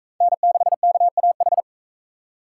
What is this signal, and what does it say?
Morse code, heard as a keyed tone, about 39 wpm, 700 Hz. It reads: N6KAH